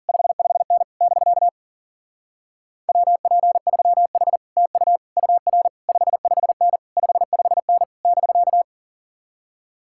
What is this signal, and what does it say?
Morse code, heard as a keyed tone, about 51 wpm, 700 Hz. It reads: 55N <BK> WP3H TU UR 55N 55N <BK>